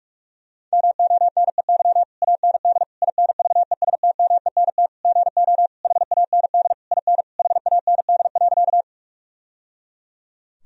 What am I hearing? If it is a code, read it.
Morse code, heard as a keyed tone, about 41 wpm, 700 Hz. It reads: MONEY AND INVESTMENT GO HAND IN HAND.
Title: Sentences from Top 1000 Words - Review Part 4 - 45wpm